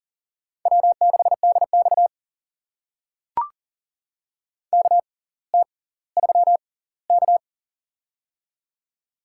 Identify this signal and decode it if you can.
Morse code, heard as a keyed tone, about 40 wpm, spaced as for 20 wpm, 700 Hz. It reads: W6DX KT3K
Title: USA Call Signs - 4 Characters - 6x Character Spacing - 40wpm